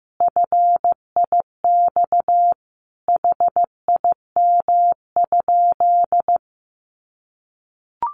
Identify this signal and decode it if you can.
Morse code, heard as a keyed tone, about 15 wpm, 700 Hz. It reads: FIX HIM?
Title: Sentences from Top 1400 Words 15wpm